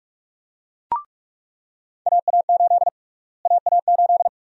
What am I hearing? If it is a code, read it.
Morse code, heard as a keyed tone, about 43 wpm, 700 Hz. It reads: AA8 AA8
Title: USA Call Sign Prefixes - Repeated 2x - 45wpm